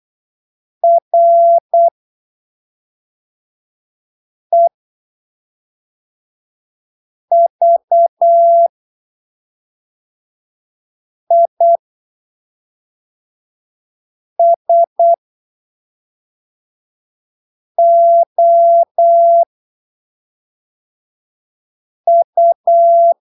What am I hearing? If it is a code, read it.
Morse code, heard as a keyed tone, about 8 wpm, spaced as for 3 wpm, 680 Hz. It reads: REVISOU
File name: arquivos / texto / texto 07 08ppm